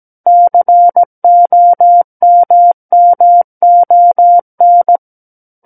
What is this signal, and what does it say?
Morse code, heard as a keyed tone, about 17 wpm, 700 Hz. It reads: COMMON
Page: SKCC Learning Center - Morse Code Practice Files